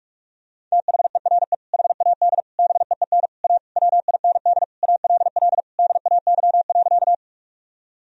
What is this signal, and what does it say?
Morse code, heard as a keyed tone, about 46 wpm, 700 Hz. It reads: THERE HAD BEEN A WIND ALL DAY.
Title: Sentences from Top 1000 Words - Review Part 2 - 45wpm